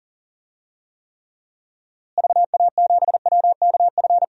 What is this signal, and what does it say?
Morse code, heard as a keyed tone, about 40 wpm, 700 Hz. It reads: VA7WKF
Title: DXCC-100 Call Signs - 6 Characters - Repeated 3x - 40wpm